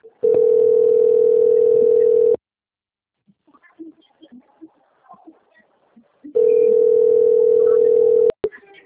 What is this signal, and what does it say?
Ringback tone (North America), 440 + 480 Hz.
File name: out- - - - - .84730